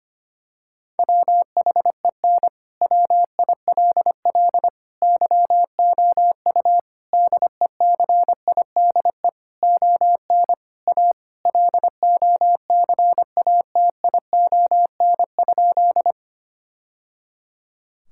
Morse code, heard as a keyed tone, about 25 wpm, 700 Hz. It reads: WHEN WILL YOU DECIDE ON A LOCATION?
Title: Sentences from Top 1000 Words - Review Part 6 - 25wpm